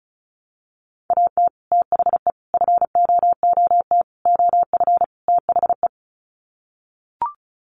Morse code, heard as a keyed tone, about 34 wpm, 700 Hz. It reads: AT THE FOOT OF THE
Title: Sets of 5 Words 35wpm